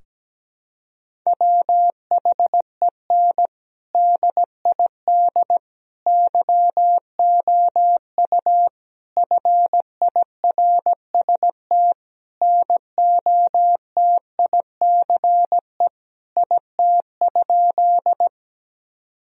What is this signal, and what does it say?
Morse code, heard as a keyed tone, about 17 wpm, 700 Hz. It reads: WHEN DID YOU FIRST NOTICE IT?
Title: Sentences from Top 1000 Words - Review Part 2 - 17wpm